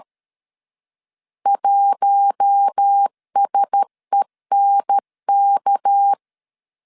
Morse code, heard as a keyed tone, about 13 wpm, 800 Hz. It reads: 1SENK
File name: LEZIONI / 6 PKX0 / 6 PK 12wpm 2006